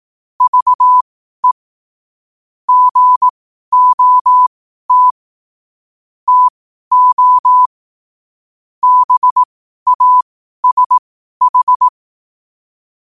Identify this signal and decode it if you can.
Morse code, heard as a keyed tone, about 18 wpm, spaced as for 12 wpm, 1000 Hz. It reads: VE GOT TO BASH
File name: qotdfiles / 12wpm